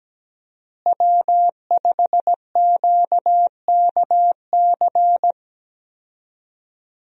Morse code, heard as a keyed tone, about 17 wpm, 700 Hz. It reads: W5QKC
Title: USA Call Signs - 5 Characters - Repeated 3x - 17wpm